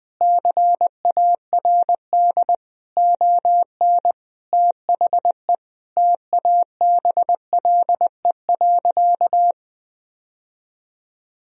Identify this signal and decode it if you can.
Morse code, heard as a keyed tone, about 21 wpm, 700 Hz. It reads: CARD ON THE TABLE.
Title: Sentences from Top 1000 Words - Review Part 4 - 20wpm